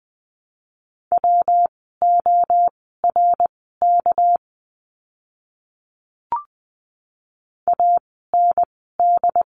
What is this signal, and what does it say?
Morse code, heard as a keyed tone, about 20 wpm, spaced as for 15 wpm, 700 Hz. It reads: WORK AND
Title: 237 Lesson 38f - Letter q - Words 20wpm